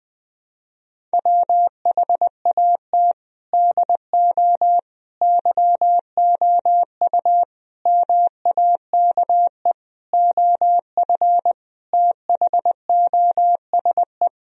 Morse code, heard as a keyed tone, about 20 wpm, 700 Hz. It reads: WHAT DO YOU MAKE OF THOSE
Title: Sentences from Top 1000 Words - Review Part 1 - 20wpm